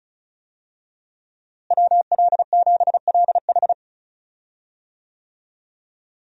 Morse code, heard as a keyed tone, about 36 wpm, 700 Hz. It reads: WL7LH
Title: USA Call Signs - 5 Characters - Repeated 2x - 35wpm